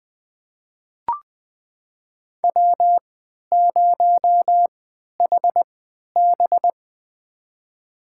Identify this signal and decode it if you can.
Morse code, heard as a keyed tone, about 20 wpm, spaced as for 11 wpm, 700 Hz. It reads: W0HB
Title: USA Call Signs - 3x Character Spacing - 20wpm